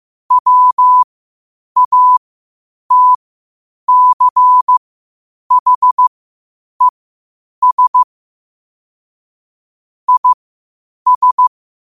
Morse code, heard as a keyed tone, about 15 wpm, spaced as for 8 wpm, 1000 Hz. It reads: WATCHES IS
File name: qotdfiles / 8wpm